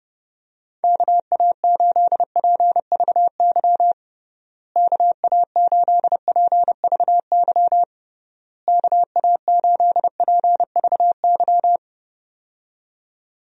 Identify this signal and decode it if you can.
Morse code, heard as a keyed tone, about 30 wpm, 700 Hz. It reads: KA8PVY KA8PVY KA8PVY
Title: USA Call Signs - 6 Characters - Repeated 3x - 30wpm